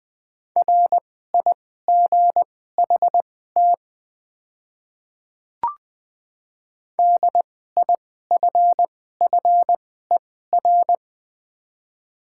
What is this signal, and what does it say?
Morse code, heard as a keyed tone, about 21 wpm, spaced as for 15 wpm, 700 Hz. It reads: RIGHT DIFFER
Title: 2x Character Spacing - Top 200 Words - 20wpm